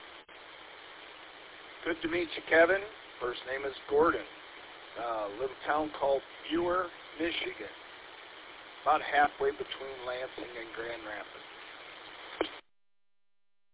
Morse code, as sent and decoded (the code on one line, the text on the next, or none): none